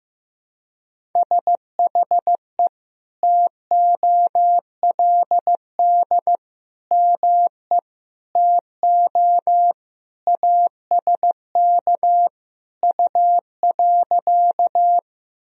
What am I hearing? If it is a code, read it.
Morse code, heard as a keyed tone, about 15 wpm, 700 Hz. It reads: SHE TOLD ME TO ASK U.
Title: Sentences from Top 400 Words and Abbreviations 15wpm